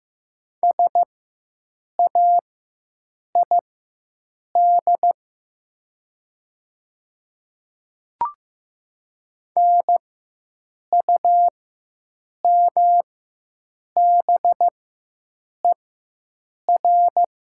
Morse code, heard as a keyed tone, about 15 wpm, spaced as for 7 wpm, 700 Hz. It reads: SAID NUMBER
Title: 4x Character Spacing - Top 100 Words - 15wpm